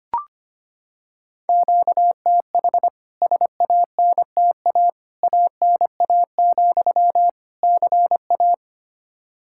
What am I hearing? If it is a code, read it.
Morse code, heard as a keyed tone, about 25 wpm, 700 Hz. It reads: QTH SANTA ANA, CA
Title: K1USN SST QSO Practice 25wpm